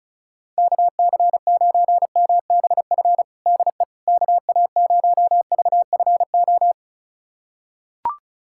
Morse code, heard as a keyed tone, about 36 wpm, 700 Hz. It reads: KC9MBF DE KA0VFO
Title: USA Callsign DE Callsign - 6 Characters - 35wpm